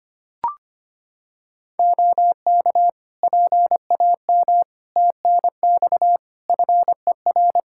Morse code, heard as a keyed tone, about 25 wpm, 700 Hz. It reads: OK PAM TNX FER
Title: K1USN SST QSO Practice 25wpm